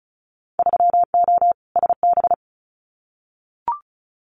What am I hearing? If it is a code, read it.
Morse code, heard as a keyed tone, about 34 wpm, 700 Hz. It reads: 3O SB
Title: Winter Field Day Exchanges Encore 35wpm